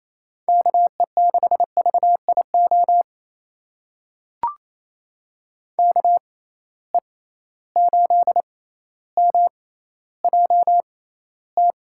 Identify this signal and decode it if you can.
Morse code, heard as a keyed tone, about 28 wpm, spaced as for 14 wpm, 700 Hz. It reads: KE6VIO KE8MJT
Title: USA Call Signs - 6 Characters - 6x Character Spacing - 28wpm